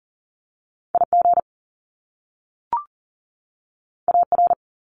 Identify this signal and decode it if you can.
Morse code, heard as a keyed tone, about 38 wpm, 700 Hz. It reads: IG AR
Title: Top Two Letter Combinations in Words 40wpm